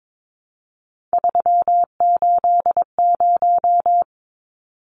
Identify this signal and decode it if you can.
Morse code, heard as a keyed tone, about 22 wpm, 700 Hz. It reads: 380
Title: Numbers - 3-Digit 22wpm